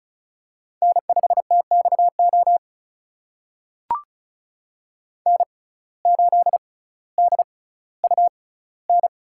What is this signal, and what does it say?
Morse code, heard as a keyed tone, about 34 wpm, spaced as for 18 wpm, 700 Hz. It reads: N5TXO N8DUN
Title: USA Call Signs - 5 Characters - 6x Character Spacing - 35wpm